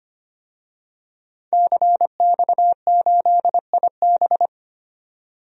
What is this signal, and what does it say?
Morse code, heard as a keyed tone, about 25 wpm, 700 Hz. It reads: CX8IB
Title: DXCC-100 Call Signs - 5 Characters - 25wpm